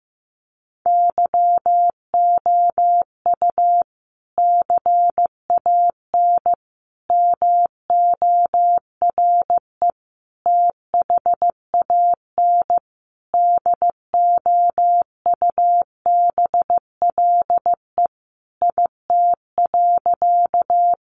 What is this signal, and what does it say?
Morse code, heard as a keyed tone, about 14 wpm, 700 Hz. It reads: YOU CAN MORE THAN DOUBLE IT.
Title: Sentences from Top 1400 Words 15wpm